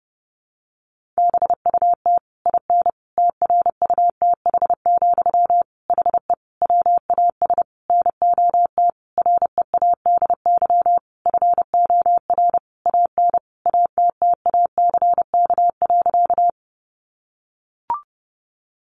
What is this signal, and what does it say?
Morse code, heard as a keyed tone, about 30 wpm, 700 Hz. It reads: BUT IN TRUTH, HE WAS NOT READY FOR AN ATTACK.
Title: Sentences from Top 1500 Words 30wpm